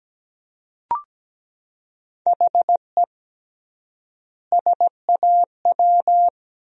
Morse code, heard as a keyed tone, about 17 wpm, 700 Hz. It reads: HE SAW
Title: 3x Word-Spacing - Sentences from Top 500 Words - 17wpm